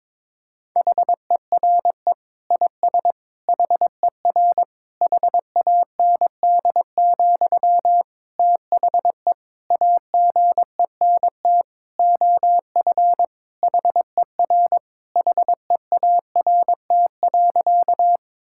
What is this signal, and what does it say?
Morse code, heard as a keyed tone, about 22 wpm, 700 Hz. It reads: HERE IS HER HAND, THE AGENT OF HER HEART.
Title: Sentences from Top 1600 Words 22wpm